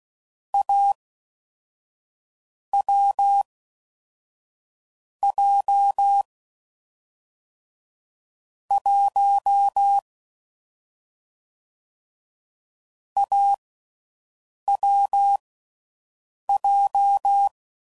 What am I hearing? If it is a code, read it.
Morse code, heard as a keyed tone, about 16 wpm, 800 Hz. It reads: A W J 1 A W J
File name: Combinations / Difficult Combos / AWJ1